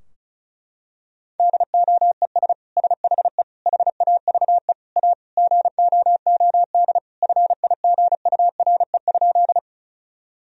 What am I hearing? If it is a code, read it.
Morse code, heard as a keyed tone, about 34 wpm, 700 Hz. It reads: DOES SHE HAVE A GOOD FIGURE?
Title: Sentences from Top 1000 Words - Review Part 2 - 35wpm